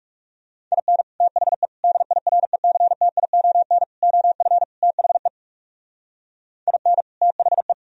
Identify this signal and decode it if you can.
Morse code, heard as a keyed tone, about 35 wpm, 700 Hz. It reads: IN THE DIRECTION OF THE IN THE
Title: Speed-Racing - Sets of 5 Words 30wpm